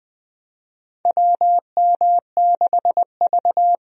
Morse code, heard as a keyed tone, about 21 wpm, 700 Hz. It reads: WM6V
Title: USA Call Signs - 4 Characters - Repeated 2x - 20wpm